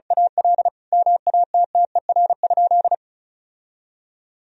Morse code, heard as a keyed tone, about 36 wpm, 700 Hz. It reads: AL MATTER?
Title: Sentences from Top 1000 Words - Review Part 4 - 35wpm